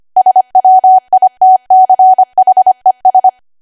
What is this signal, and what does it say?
Morse code, heard as a keyed tone, about 25 wpm, 750 Hz. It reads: SWITCHES
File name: cwa / W1AW-Level3 / 25WPM